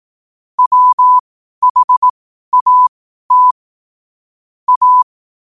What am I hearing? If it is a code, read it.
Morse code, heard as a keyed tone, about 18 wpm, spaced as for 12 wpm, 1000 Hz. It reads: WHAT A
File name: qotdfiles / 12wpm